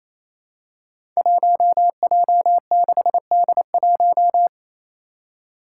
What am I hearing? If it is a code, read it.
Morse code, heard as a keyed tone, about 28 wpm, 700 Hz. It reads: 1J6D1
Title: Five Challenging Letter-Number Combinations 28wpm